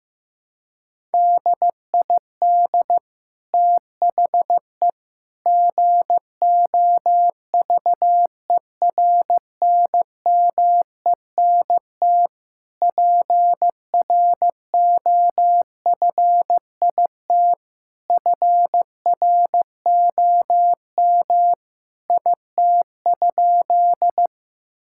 Morse code, heard as a keyed tone, about 15 wpm, 700 Hz. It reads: DID THE GOVERNMENT PROFIT FROM IT?
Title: Sentences from Top 1000 Words - Review Part 5 - 15wpm